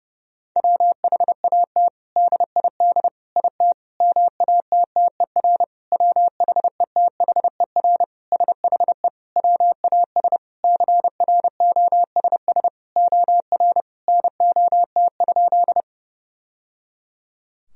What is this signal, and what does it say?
Morse code, heard as a keyed tone, about 30 wpm, 700 Hz. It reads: WHAT DID IT MATTER WHETHER SHE WAS CROSS OR NOT?
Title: Sentences from Top 1000 Words - Review Part 3 - 30wpm